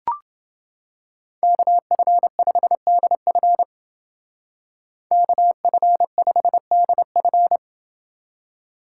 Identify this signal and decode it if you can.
Morse code, heard as a keyed tone, about 29 wpm, 700 Hz. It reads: KF5DF KF5DF